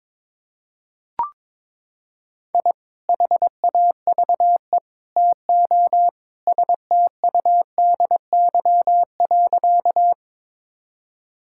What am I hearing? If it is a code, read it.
Morse code, heard as a keyed tone, about 22 wpm, 700 Hz. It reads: I HAVE TO STUDY.
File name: practice / 22wpm / Sentences with 3-Word Phrases - 22wpm